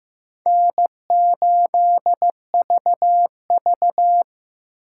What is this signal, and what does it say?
Morse code, heard as a keyed tone, about 18 wpm, 700 Hz. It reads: N8VV
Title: Speed Racing - USA 1x2 Call Signs 15wpm